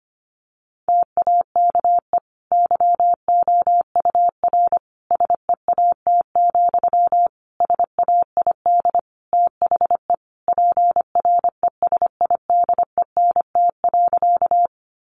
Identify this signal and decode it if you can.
Morse code, heard as a keyed tone, about 25 wpm, 700 Hz. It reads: TAKE YOUR SEAT, SAID THE PRESIDENT.
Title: Sentences from Top 1500 Words 25wpm